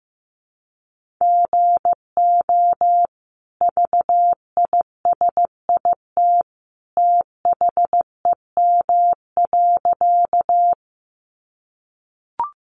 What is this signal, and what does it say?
Morse code, heard as a keyed tone, about 15 wpm, 700 Hz. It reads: GO VISIT THEM.
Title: Sentences from Top 900 Words 15wpm